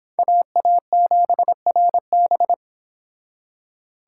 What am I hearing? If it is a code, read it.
Morse code, heard as a keyed tone, about 30 wpm, 700 Hz. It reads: AA7RB